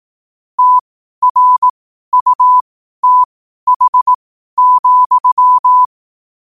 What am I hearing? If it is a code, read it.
Morse code, heard as a keyed tone, about 18 wpm, spaced as for 12 wpm, 1000 Hz. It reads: TRUTH,
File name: qotdfiles / 12wpm